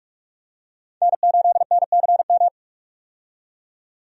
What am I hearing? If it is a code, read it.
Morse code, heard as a keyed tone, about 44 wpm, 700 Hz. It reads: N8NCM
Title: USA Call Signs - 5 Characters - 45wpm